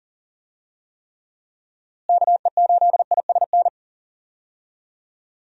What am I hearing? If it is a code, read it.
Morse code, heard as a keyed tone, about 40 wpm, 700 Hz. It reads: KE8ISN
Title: USA Call Signs - 6 Characters - Repeated 2x - 40wpm